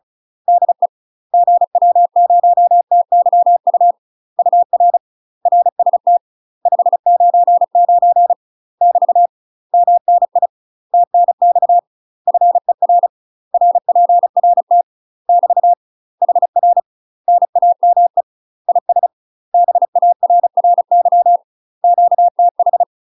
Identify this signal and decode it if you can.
Morse code, heard as a keyed tone, about 34 wpm, 700 Hz. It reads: DE GW0TYU UR RST 599 <BT> MNI TNX FER RPRT <BT> HR NAME IS BARRY QTH
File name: UK MASTER QSOs TRAINING / 35WPM / UK MASTER QSOs 0025